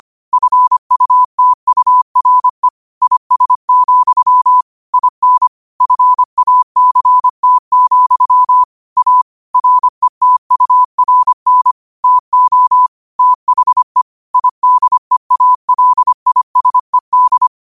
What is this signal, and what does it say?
Morse code, heard as a keyed tone, about 25 wpm, 1000 Hz. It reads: FUTURE IS, IN FACT, A RETURN TO THE IDEALISED
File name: qotdfiles / 25wpm